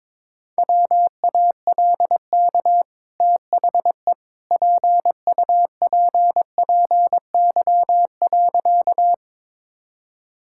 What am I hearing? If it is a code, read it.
Morse code, heard as a keyed tone, about 22 wpm, 690 Hz. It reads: WALK THE PUPPY.
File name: practice / 22wpm / Easy Sentences with Hints - 22wpm